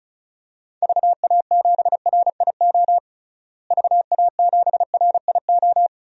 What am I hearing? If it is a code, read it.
Morse code, heard as a keyed tone, about 36 wpm, 700 Hz. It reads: VA7RIO VA7RIO
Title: DXCC-100 Call Signs - 6 Characters - Repeated 2x - 35wpm